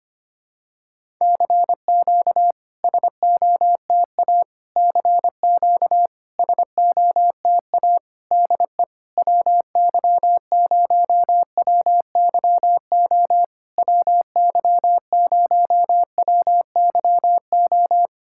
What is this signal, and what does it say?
Morse code, heard as a keyed tone, about 25 wpm, 690 Hz. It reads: CQ SOTA CQ SOTA DE WY0WYO WY0WYO
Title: SOTA QSO Practice 25wpm